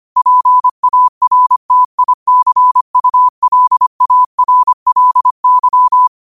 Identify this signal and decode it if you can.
Morse code, heard as a keyed tone, about 25 wpm, 1000 Hz. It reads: PARTICULARLY
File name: qotdfiles / 25wpm